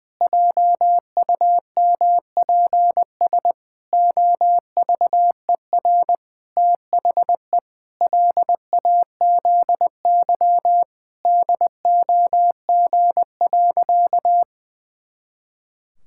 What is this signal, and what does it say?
Morse code, heard as a keyed tone, about 21 wpm, 700 Hz. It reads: JUMPS OVER THE LAZY DOG.
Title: Pangram Challenge v2 - 20wpm